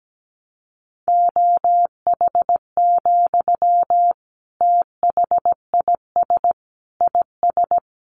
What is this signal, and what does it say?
Morse code, heard as a keyed tone, about 17 wpm, 700 Hz. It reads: OH, THIS IS
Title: Sentences from Top 1100 Words 17wpm